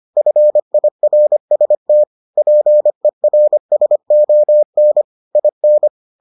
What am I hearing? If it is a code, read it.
Morse code, heard as a keyed tone, about 25 wpm, 600 Hz. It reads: FIRST PERSON IN